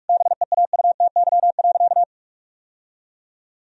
Morse code, heard as a keyed tone, about 41 wpm, 700 Hz. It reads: BEAUTY.
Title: Sentences from Top 1000 Words - Review Part 2 - 45wpm